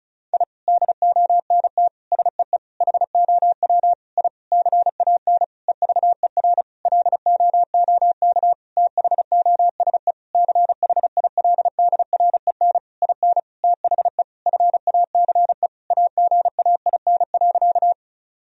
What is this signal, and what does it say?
Morse code, heard as a keyed tone, about 34 wpm, 700 Hz. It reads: I DONT SEE HOW I CAN EVER LOOK THOSE CHILDREN IN THE FACE AGAIN.
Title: Sentences from Top 1000 Words - Review Part 2 - 35wpm